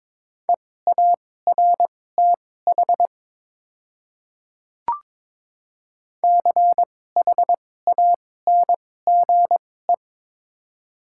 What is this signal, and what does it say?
Morse code, heard as a keyed tone, about 22 wpm, spaced as for 16 wpm, 700 Hz. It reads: EARTH CHANGE
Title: 2x Character Spacing - Top 200 Words - 22wpm